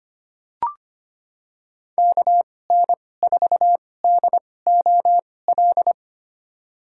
Morse code, heard as a keyed tone, about 25 wpm, spaced as for 18 wpm, 700 Hz. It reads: KN4DOL
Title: USA Call Signs - 2x Character Spacing - 25wpm